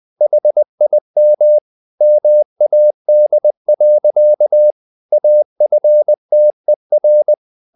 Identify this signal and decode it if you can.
Morse code, heard as a keyed tone, about 20 wpm, 600 Hz. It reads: HIM MAD. AFTER
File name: the old man and the sea part 2 20wpm